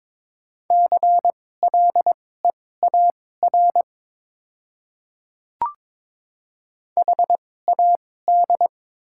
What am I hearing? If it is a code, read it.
Morse code, heard as a keyed tone, about 22 wpm, spaced as for 16 wpm, 700 Hz. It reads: CLEAR HAD
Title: 2x Character Spacing - Top 500 Words - 22wpm